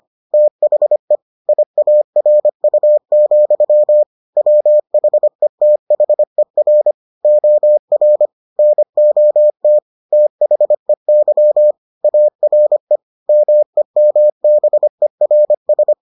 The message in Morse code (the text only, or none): THE IARU, WHETHER OR NOT THEY ARE MEMBERS